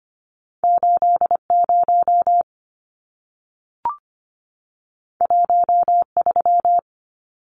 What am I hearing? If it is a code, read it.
Morse code, heard as a keyed tone, about 25 wpm, 700 Hz. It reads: 80 13
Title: Numbers - 2-Digit 25wpm